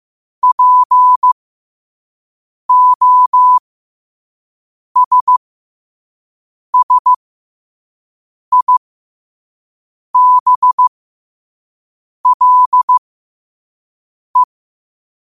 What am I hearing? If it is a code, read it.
Morse code, heard as a keyed tone, about 15 wpm, spaced as for 5 wpm, 1000 Hz. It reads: POSSIBLE
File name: qotdfiles / 5wpm